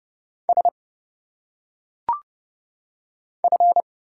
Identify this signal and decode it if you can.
Morse code, heard as a keyed tone, about 30 wpm, 700 Hz. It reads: S F
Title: Single Letters - Phonetic Pronunciation 30wpm